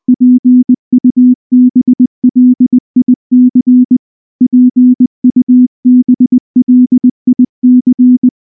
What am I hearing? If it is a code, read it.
Morse code, heard as a keyed tone, about 19 wpm, 250 Hz. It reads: PUBLIC PUBLIC
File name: morse / morse-training-20WPM